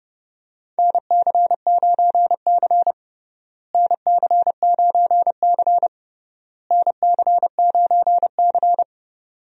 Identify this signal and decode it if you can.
Morse code, heard as a keyed tone, about 30 wpm, 700 Hz. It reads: NC9C NC9C NC9C
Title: USA Call Signs - 4 Characters - Repeated 3x - 30wpm